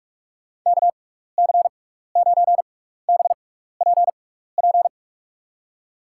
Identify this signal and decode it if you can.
Morse code, heard as a keyed tone, about 47 wpm, spaced as for 16 wpm, 700 Hz. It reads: KC9BPP
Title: USA Call Signs - 6 Characters - 6x Character Spacing - 45wpm